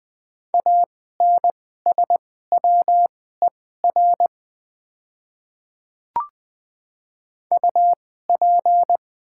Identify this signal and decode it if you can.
Morse code, heard as a keyed tone, about 20 wpm, spaced as for 15 wpm, 700 Hz. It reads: ANSWER UP
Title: 2x Character Spacing - Top 300 Words - 20wpm